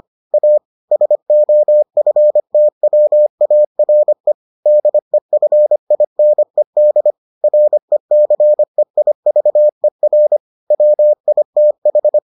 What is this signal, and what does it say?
Morse code, heard as a keyed tone, about 25 wpm, 600 Hz. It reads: A SOFTWARE DEFINED RECEIVER WITH